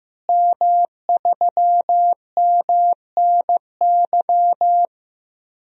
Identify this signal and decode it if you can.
Morse code, heard as a keyed tone, about 15 wpm, 700 Hz. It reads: M3MNY
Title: DXCC-100 Call Signs - 5 Characters - Repeated 3x - 15wpm